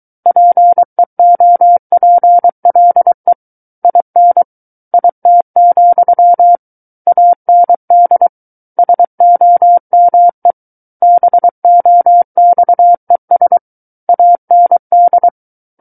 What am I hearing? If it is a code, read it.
Morse code, heard as a keyed tone, about 23 wpm, 700 Hz. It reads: PEOPLE IN IT, AND SOME BOXES AND
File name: Book One / 14-Chapter 14 - 23 WPM